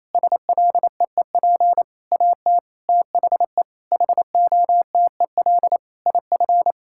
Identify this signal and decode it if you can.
Morse code, heard as a keyed tone, about 27 wpm, 700 Hz. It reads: SLEEP AT THE HOTEL IF
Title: Sentences from Top 1600 Words 28wpm